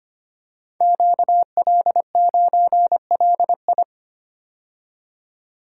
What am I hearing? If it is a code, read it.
Morse code, heard as a keyed tone, about 25 wpm, 700 Hz. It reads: QL9LI
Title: Code Groups - 5 Letter:Numbers - 25wpm